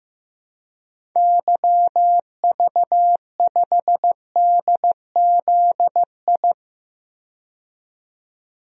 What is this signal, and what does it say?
Morse code, heard as a keyed tone, about 16 wpm, 700 Hz. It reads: YV5DZI
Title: DXCC-100 Call Signs - 6 Characters - 15wpm